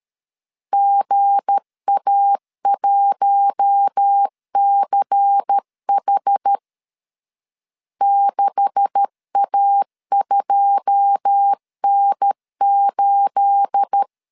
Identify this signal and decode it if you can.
Morse code, heard as a keyed tone, about 13 wpm, 800 Hz. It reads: GA1CH 6A2N8
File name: CW / DET / 1 9 1500C S A 12wpm 3800